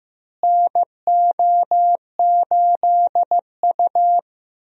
Morse code, heard as a keyed tone, about 16 wpm, 700 Hz. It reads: NO8U
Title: USA Call Signs - 4 Characters - Repeated 3x - 15wpm